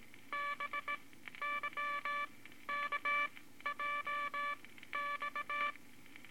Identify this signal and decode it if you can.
Morse code, heard as a keyed tone, about 19 wpm, spaced as for 13 wpm, 1160 Hz, with noise. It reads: BYKJX
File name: Morse Code Records / 18 B LCM 165 Ten Words Per Minute